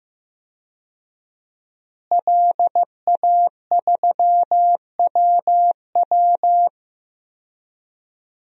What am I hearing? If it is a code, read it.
Morse code, heard as a keyed tone, about 15 wpm, 700 Hz. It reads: LA3WW
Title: DXCC-100 Call Signs - 5 Characters - Repeated 2x - 15wpm